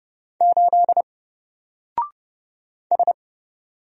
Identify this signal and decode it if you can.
Morse code, heard as a keyed tone, about 30 wpm, 700 Hz. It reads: 8 S
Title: Single Letter-Number Alt 30wpm